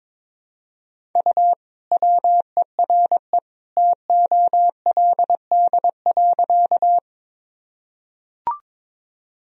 Morse code, heard as a keyed tone, about 22 wpm, 700 Hz. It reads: U WERE TOLD.
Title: Sentences from Top 400 Words and Abbreviations 22wpm